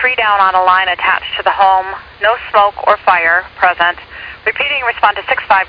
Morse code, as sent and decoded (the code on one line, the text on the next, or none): none